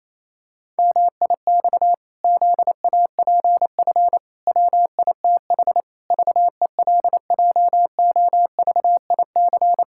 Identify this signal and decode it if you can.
Morse code, heard as a keyed tone, about 28 wpm, 700 Hz. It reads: MIX ZAPF WITH VELJOVIC
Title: Pangram Challenge v2 - 28wpm